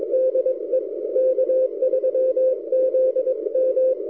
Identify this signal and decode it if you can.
Morse code, heard as a keyed tone, about 21 wpm, 520 Hz. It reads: DE K3ZM
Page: during this recording I switched the rx loop direction front to the backside K3ZM almost disappeared so my K9AY loop seems to work okay recorded amazing 160m signal